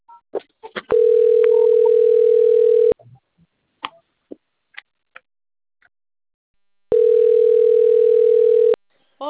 Ringback tone (North America), 440 + 480 Hz.
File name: out- - - - - .66740